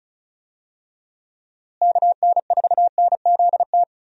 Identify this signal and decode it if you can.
Morse code, heard as a keyed tone, about 34 wpm, 700 Hz. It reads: KN4NZT
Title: USA Call Signs - 6 Characters - Repeated 3x - 35wpm